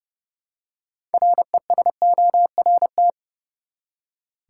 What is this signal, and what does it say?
Morse code, heard as a keyed tone, about 30 wpm, 700 Hz. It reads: RESORT
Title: Six Letter Words - 30wpm